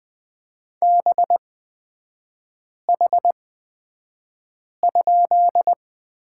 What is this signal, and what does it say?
Morse code, heard as a keyed tone, about 20 wpm, spaced as for 5 wpm, 700 Hz. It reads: BH?
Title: 255 Lesson 41e - Period - Three Characters 20wpm